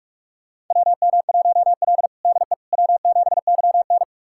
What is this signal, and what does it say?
Morse code, heard as a keyed tone, about 46 wpm, 700 Hz. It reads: WM1L DE W7YN
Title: USA Callsign DE Callsign - 4 Characters - 45wpm